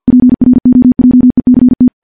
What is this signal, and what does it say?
Morse code, heard as a keyed tone, about 50 wpm, 250 Hz. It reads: PROJECT
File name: morse-training-50WPM